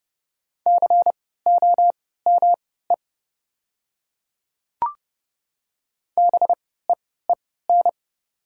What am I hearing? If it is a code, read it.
Morse code, heard as a keyed tone, about 30 wpm, spaced as for 17 wpm, 700 Hz. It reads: COME BEEN
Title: 3x Character Spacing - Top 100 Words - 30wpm